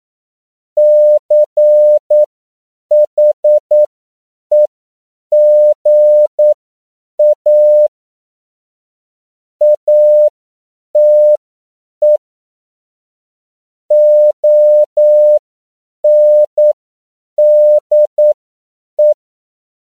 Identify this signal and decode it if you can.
Morse code, heard as a keyed tone, about 9 wpm, spaced as for 7 wpm, 600 Hz. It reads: CHEGA ATE ONDE
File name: CW / 10PPM